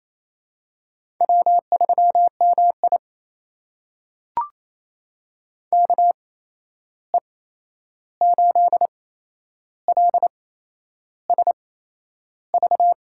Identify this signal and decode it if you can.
Morse code, heard as a keyed tone, about 28 wpm, spaced as for 11 wpm, 700 Hz. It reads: W3MI KE8LSV
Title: USA Call Signs - 8x Character Spacing - 28wpm